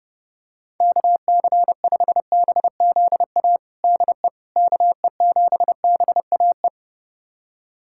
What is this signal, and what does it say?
Morse code, heard as a keyed tone, about 30 wpm, 700 Hz. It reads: KC5BZA DE KE7BAE
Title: USA Callsign DE Callsign 30wpm